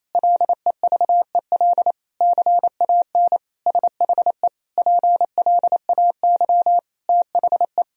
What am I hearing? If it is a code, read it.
Morse code, heard as a keyed tone, about 28 wpm, 700 Hz. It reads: LEVEL CAN SHE PLAY THE
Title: Sentences from Top 1000 Words - Review Part 4 - 28wpm